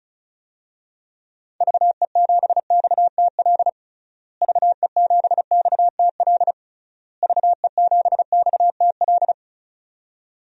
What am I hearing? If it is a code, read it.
Morse code, heard as a keyed tone, about 34 wpm, 700 Hz. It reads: VE7XTL VE7XTL VE7XTL
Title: DXCC-100 Call Signs - 6 Characters - Repeated 3x - 35wpm